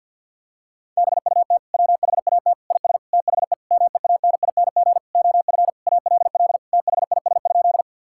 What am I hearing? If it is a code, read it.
Morse code, heard as a keyed tone, about 50 wpm, 700 Hz. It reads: BUT WHAT IS THE MEANING OF ALL THIS?
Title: Sentences from Top 1000 Words - Review Part 6 - 50wpm